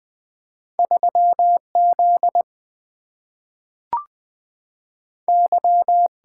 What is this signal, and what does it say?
Morse code, heard as a keyed tone, about 20 wpm, 700 Hz. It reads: 3 Z Y
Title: Single-Letter Single-Number 20wpm